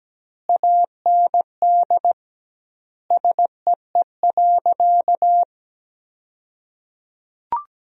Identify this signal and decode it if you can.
Morse code, heard as a keyed tone, about 17 wpm, 700 Hz. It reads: AND SEE.
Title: 2x Word-Spacing - Sentences from Top 100 Words - 17wpm